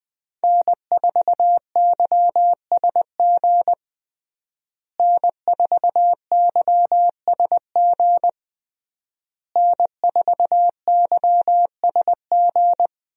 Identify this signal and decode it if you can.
Morse code, heard as a keyed tone, about 20 wpm, 700 Hz. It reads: N4YSG N4YSG N4YSG
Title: USA Call Signs - 5 Characters - Repeated 3x - 20wpm